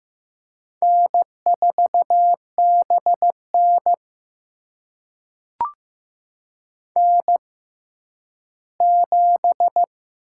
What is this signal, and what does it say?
Morse code, heard as a keyed tone, about 15 wpm, spaced as for 10 wpm, 700 Hz. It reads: N4BN N7
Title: USA Call Signs - 6x Character Spacing - 15wpm